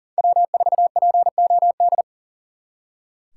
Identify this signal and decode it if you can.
Morse code, heard as a keyed tone, about 52 wpm, 700 Hz. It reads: W4POD